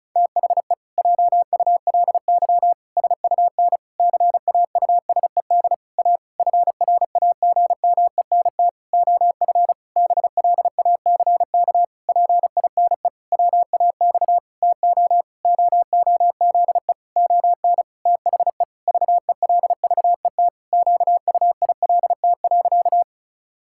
Morse code, heard as a keyed tone, about 36 wpm, 700 Hz. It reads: THE JULY SUN CAUSED A FRAGMENT OF BLACK PINE WAX TO OOZE ON THE VELVET QUILT.
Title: Pangram Challenge v2 - 35wpm